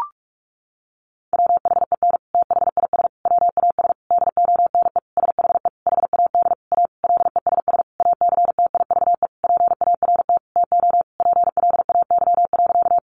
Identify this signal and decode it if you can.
Morse code, heard as a keyed tone, about 44 wpm, 700 Hz. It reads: WHEN THIS WAS DONE SHE HAD A LESS ACTIVE PART TO PLAY.
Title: Sentences from Top 900 Words 45wpm